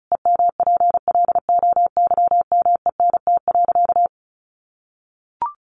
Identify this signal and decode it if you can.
Morse code, heard as a keyed tone, about 34 wpm, 700 Hz. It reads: EMPLOYMENT.
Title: Sentences from Top 1500 Words 35wpm